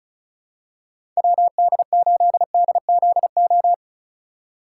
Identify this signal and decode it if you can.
Morse code, heard as a keyed tone, about 36 wpm, 700 Hz. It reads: WD8DZO
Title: USA Call Signs - 6 Characters - 2x Character Spacing - 35wpm